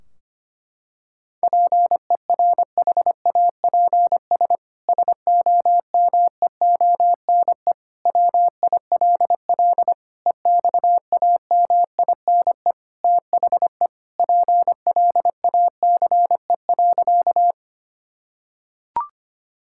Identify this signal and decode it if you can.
Morse code, heard as a keyed tone, about 25 wpm, 700 Hz. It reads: PERHAPS SOMEONE WILL EXAMINE THE PLACE.
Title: Sentences from Top 1600 Words 25wpm